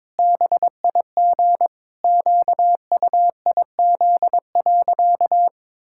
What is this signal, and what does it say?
Morse code, heard as a keyed tone, about 22 wpm, 700 Hz. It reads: BIG QUIZ.
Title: Pangram Challenge v2 - 22wpm